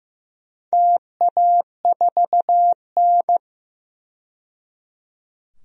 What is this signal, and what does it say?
Morse code, heard as a keyed tone, about 15 wpm, 700 Hz. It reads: TA4N
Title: DXCC-100 Call Signs - 4 Characters - 15wpm